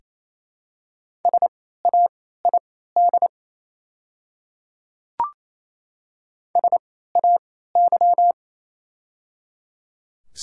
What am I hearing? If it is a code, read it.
Morse code, heard as a keyed tone, about 28 wpm, spaced as for 16 wpm, 700 Hz. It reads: SAID SAY
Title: 3x Character Spacing - Top 200 Words - 28wpm